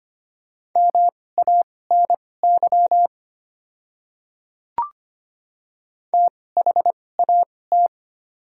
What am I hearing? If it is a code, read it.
Morse code, heard as a keyed tone, about 25 wpm, spaced as for 18 wpm, 690 Hz. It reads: MANY THAT
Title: 2x Character Spacing - Top 100 Words - 25wpm